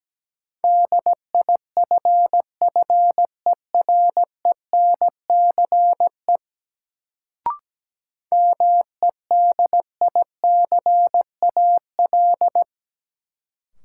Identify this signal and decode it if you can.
Morse code, heard as a keyed tone, about 17 wpm, 700 Hz. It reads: DIFFERENCE MEDICAL
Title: Sentences from Top 800 Words - New Word Drill - 17wpm